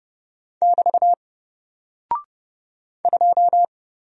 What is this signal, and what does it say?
Morse code, heard as a keyed tone, about 30 wpm, 700 Hz. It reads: <BT> 2
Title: Single Character (Letter:Number:Punctuation:Prosign) Alt 30wpm